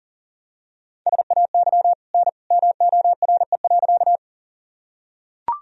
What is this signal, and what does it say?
Morse code, heard as a keyed tone, about 40 wpm, 700 Hz. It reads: SAY N MORE.
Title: Sentences from Top 200 Words and Abbreviations 40wpm